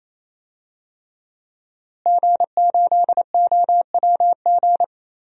Morse code, heard as a keyed tone, about 28 wpm, 700 Hz. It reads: G8OWG
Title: DXCC-100 Call Signs - 5 Characters - Repeated 3x - 28wpm